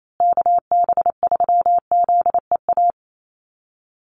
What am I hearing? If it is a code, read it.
Morse code, heard as a keyed tone, about 28 wpm, 700 Hz. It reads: KB3ZEA
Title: USA Call Signs Repeated 3x 28wpm